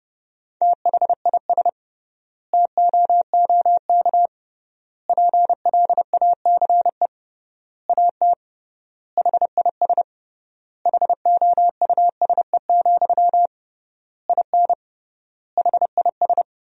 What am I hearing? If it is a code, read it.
Morse code, heard as a keyed tone, about 30 wpm, 700 Hz. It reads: THIS TOOK PLACE AT HIS HOUSE, IN HIS
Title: 3x Word-Spacing - Sentences from Top 300 Words - 30wpm